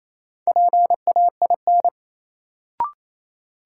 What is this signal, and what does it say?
Morse code, heard as a keyed tone, about 28 wpm, 700 Hz. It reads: PAIN
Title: Sentences from Top 1600 Words - New Word Drill - 28wpm